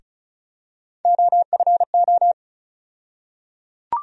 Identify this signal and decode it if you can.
Morse code, heard as a keyed tone, about 36 wpm, 700 Hz. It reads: OFO
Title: USA Call Sign Postfixes - 3 Letters 35wpm